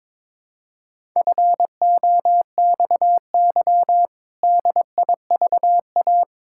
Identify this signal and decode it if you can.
Morse code, heard as a keyed tone, about 22 wpm, 700 Hz. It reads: FOXY DIVA
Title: Pangram Challenge v2 - 22wpm